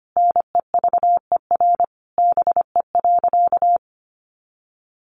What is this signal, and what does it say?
Morse code, heard as a keyed tone, about 25 wpm, 700 Hz. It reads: NEVER BE.
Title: Sentences from Top 300 Words 25wpm